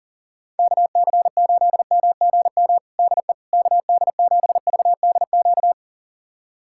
Morse code, heard as a keyed tone, about 37 wpm, 700 Hz. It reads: KC8MGM DE KD7VDQ
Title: USA Callsign DE Callsign - 6 Characters - 40wpm